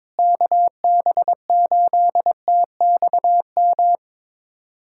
Morse code, heard as a keyed tone, about 22 wpm, 700 Hz. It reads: KB8TXM
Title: USA Callsign DE Callsign 22wpm